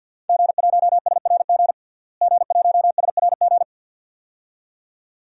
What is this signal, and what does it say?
Morse code, heard as a keyed tone, about 49 wpm, 700 Hz. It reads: G1SRG G1SRG
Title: DXCC-100 Call Signs - 5 Characters - Repeated 2x - 50wpm